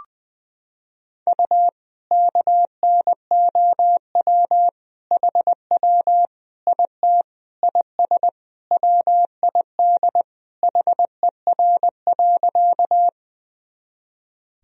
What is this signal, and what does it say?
Morse code, heard as a keyed tone, about 20 wpm, 700 Hz. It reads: U KNOW HW IT IS WID HER.
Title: Sentences from Top 100 Words and Abbreviations 20wpm